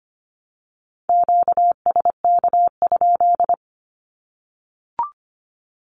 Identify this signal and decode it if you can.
Morse code, heard as a keyed tone, about 25 wpm, 700 Hz. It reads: QSK?
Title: Q-Codes Repeated 3x 25wpm